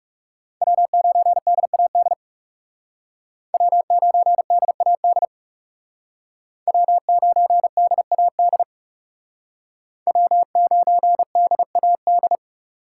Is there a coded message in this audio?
Yes: Morse code from 0.6 to 12.4 s, heard as a keyed tone, about 38 wpm, 700 Hz. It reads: W9DAD W9DAD W9DAD W9DAD